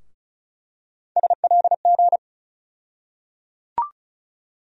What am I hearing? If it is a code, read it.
Morse code, heard as a keyed tone, about 35 wpm, 700 Hz. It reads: SLG
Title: USA Call Sign Postfixes - 3 Letters 35wpm